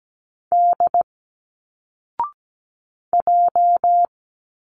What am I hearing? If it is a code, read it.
Morse code, heard as a keyed tone, about 17 wpm, 700 Hz. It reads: D J
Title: Single Letters 17wpm